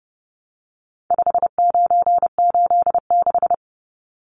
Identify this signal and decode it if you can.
Morse code, heard as a keyed tone, about 30 wpm, 700 Hz. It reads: 5986
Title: Numbers - 4-Digit 30wpm